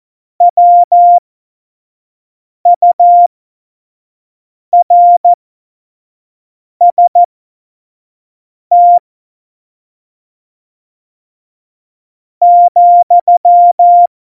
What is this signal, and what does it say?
Morse code, heard as a keyed tone, about 14 wpm, spaced as for 4 wpm, 700 Hz. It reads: WURST,
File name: Morsekurs / Witze / Witz 27. November 2018, 25 F70